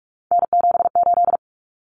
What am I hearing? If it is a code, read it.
Morse code, heard as a keyed tone, about 44 wpm, 700 Hz. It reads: N78
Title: Common Cut Numbers - 3-Digit 45wpm